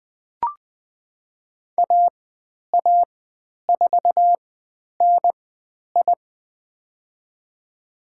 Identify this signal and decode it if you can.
Morse code, heard as a keyed tone, about 20 wpm, spaced as for 10 wpm, 700 Hz. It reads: AA4NI
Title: 102 Lesson 17g - Letter u - Call Signs 20wpm